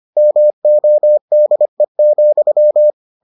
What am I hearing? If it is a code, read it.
Morse code, heard as a keyed tone, about 25 wpm, 600 Hz. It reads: MODE,